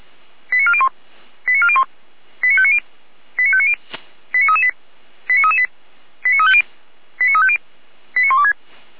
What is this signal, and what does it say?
Five-tone selective call (ZVEI), heard as quick runs of five tones, tones of about 70 ms. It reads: 89401 89401 89590 89590 89338 89338 89350 89350 89126
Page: SDR/ZVEI